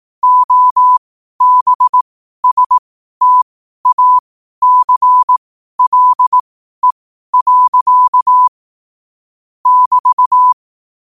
Morse code, heard as a keyed tone, about 18 wpm, spaced as for 12 wpm, 1000 Hz. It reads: OBSTACLE. <BT>
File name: qotdfiles / 12wpm